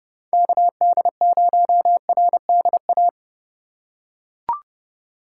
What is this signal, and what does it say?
Morse code, heard as a keyed tone, about 30 wpm, 700 Hz. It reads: KD0RDA
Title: USA Call Signs - 2x Character Spacing - 30wpm